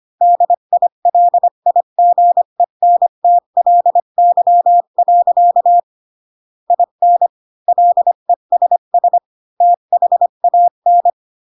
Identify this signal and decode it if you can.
Morse code, heard as a keyed tone, about 25 wpm, 700 Hz. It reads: DILIGENTLY. IN LESS THAN
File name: Pinocchio / Pinocchio ch8 20 25 ik1hge